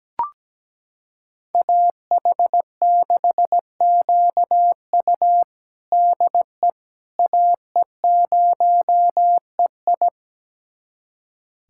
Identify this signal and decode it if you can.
Morse code, heard as a keyed tone, about 17 wpm, 700 Hz. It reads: AH6QU DE AE0EI
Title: USA Callsign DE Callsign - 5 Characters - 17wpm